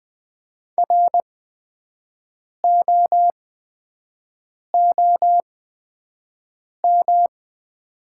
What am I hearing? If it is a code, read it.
Morse code, heard as a keyed tone, about 20 wpm, spaced as for 5 wpm, 700 Hz. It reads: ROOM
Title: 8x Character Spacing - Top 300 Words - 20wpm